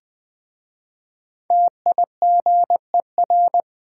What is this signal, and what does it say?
Morse code, heard as a keyed tone, about 20 wpm, 700 Hz. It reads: TIGER
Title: Easy Words with Hints - 20wpm